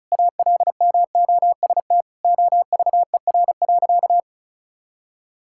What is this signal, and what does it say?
Morse code, heard as a keyed tone, about 36 wpm, 700 Hz. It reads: ALMOST OVER.
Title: Sentences from Top 1600 Words 35wpm